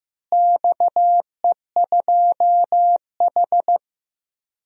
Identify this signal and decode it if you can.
Morse code, heard as a keyed tone, about 15 wpm, 700 Hz. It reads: XE2H
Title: DXCC-100 Call Signs - 4 Characters - Repeated 2x - 15wpm